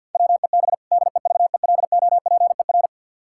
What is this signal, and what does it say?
Morse code, heard as a keyed tone, about 50 wpm, 700 Hz. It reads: WEB DEVELOPER
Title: Practice QSO - Exchange 3 - More Info - 50wpm